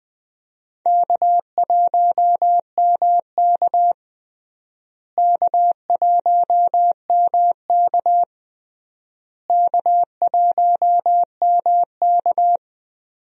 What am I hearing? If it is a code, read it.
Morse code, heard as a keyed tone, about 20 wpm, 700 Hz. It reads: K1MK K1MK K1MK
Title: USA Call Signs - 4 Characters - Repeated 3x - 20wpm